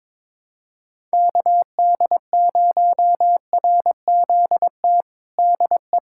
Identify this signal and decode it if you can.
Morse code, heard as a keyed tone, about 21 wpm, 700 Hz. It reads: KD0RZT DE
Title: USA Callsign DE Callsign 22wpm